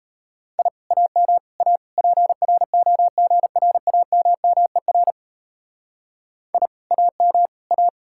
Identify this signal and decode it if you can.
Morse code, heard as a keyed tone, about 32 wpm, 700 Hz. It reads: I AM A PROGRAMMER I AM A
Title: Speed-Racing Common QSO Phrases Encore - 28wpm